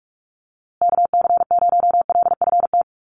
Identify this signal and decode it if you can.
Morse code, heard as a keyed tone, about 44 wpm, 700 Hz. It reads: KC0LFT
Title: USA Call Signs Repeated 2x 45wpm